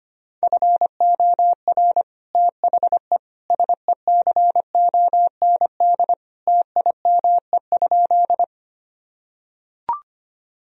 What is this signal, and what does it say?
Morse code, heard as a keyed tone, about 25 wpm, 700 Hz. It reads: FOR THE SECOND TIME?
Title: Sentences from Top 1000 Words - Review Part 1 - 25wpm